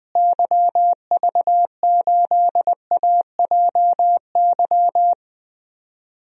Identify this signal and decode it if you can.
Morse code, heard as a keyed tone, about 20 wpm, 700 Hz. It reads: YV8AJY
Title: DXCC-100 Call Signs - 6 Characters - Repeated 3x - 20wpm